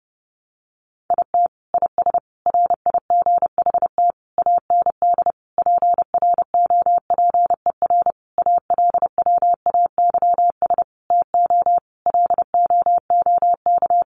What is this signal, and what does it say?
Morse code, heard as a keyed tone, about 30 wpm, 700 Hz. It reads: IT IS RIGHT AND PROPER ALWAYS TO LOOK
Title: Sentences from Top 1500 Words 30wpm